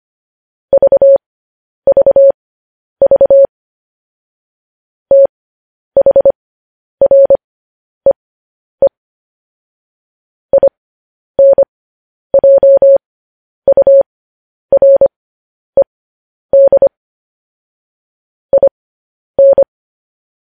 Morse code, heard as a keyed tone, about 25 wpm, spaced as for 10 wpm, 550 Hz. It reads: VVV THREE INJURED IN
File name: podcast / News headlines-1x-25x10